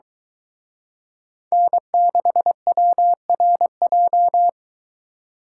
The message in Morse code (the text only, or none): N6WRJ